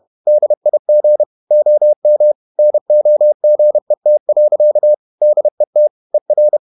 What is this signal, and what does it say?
Morse code, heard as a keyed tone, about 31 wpm, 600 Hz. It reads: DIG OM NOGET. DET ER
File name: Ferie QRQ 31wpm